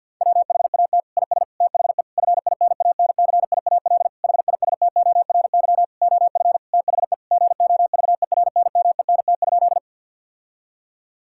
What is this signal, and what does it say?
Morse code, heard as a keyed tone, about 52 wpm, 700 Hz. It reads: WHAT IS THE FINANCIAL HISTORY OF THE GOVERNMENT?
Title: Sentences from Top 1000 Words - Review Part 5 - 50wpm